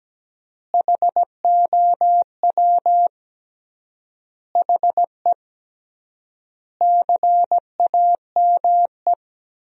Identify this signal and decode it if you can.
Morse code, heard as a keyed tone, about 17 wpm, 700 Hz. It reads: HOW HE CAME
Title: 3x Word-Spacing - Sentences from Top 500 Words - 17wpm